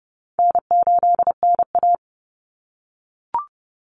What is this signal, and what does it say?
Morse code, heard as a keyed tone, about 30 wpm, 700 Hz. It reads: N8NA
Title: Call Signs - 30wpm